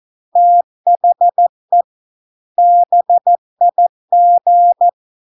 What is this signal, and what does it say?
Morse code, heard as a keyed tone, about 14 wpm, 700 Hz. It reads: THE BIG